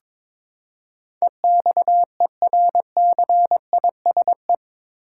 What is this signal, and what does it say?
Morse code, heard as a keyed tone, about 22 wpm, 700 Hz. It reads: EXERCISE
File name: practice / 22wpm / Eight Letter Words - 22wpm